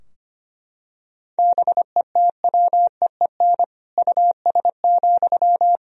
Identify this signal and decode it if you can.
Morse code, heard as a keyed tone, about 25 wpm, 700 Hz. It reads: BETWEEN US,
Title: Sentences from Top 1000 Words - Review Part 2 - 25wpm